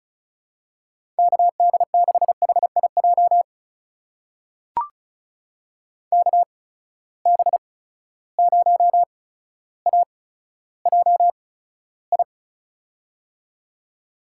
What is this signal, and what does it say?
Morse code, heard as a keyed tone, about 36 wpm, spaced as for 15 wpm, 700 Hz. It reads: KD6HIJ KB0AJI
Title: USA Call Signs - 6 Characters - 8x Character Spacing - 35wpm